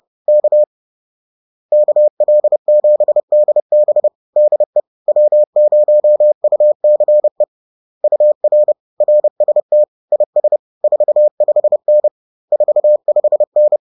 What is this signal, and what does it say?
Morse code, heard as a keyed tone, about 30 wpm, 600 Hz. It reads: K KL7DB DE W0UCE UR RST IS 45N 45N